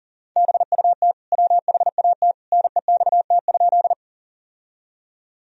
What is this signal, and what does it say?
Morse code, heard as a keyed tone, about 40 wpm, 700 Hz. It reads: BUT WHAT NEXT?
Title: Sentences from Top 1000 Words - Review Part 1 - 40wpm